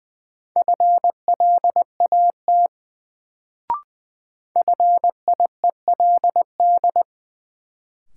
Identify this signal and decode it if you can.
Morse code, heard as a keyed tone, about 21 wpm, 700 Hz. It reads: FLAT FIELD
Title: Sentences from Top 1000 Words - New Word Drill - 20wpm